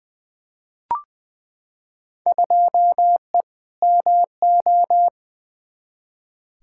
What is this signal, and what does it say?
Morse code, heard as a keyed tone, about 21 wpm, 700 Hz. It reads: 2E MO
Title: Field Day Exchanges - 20wpm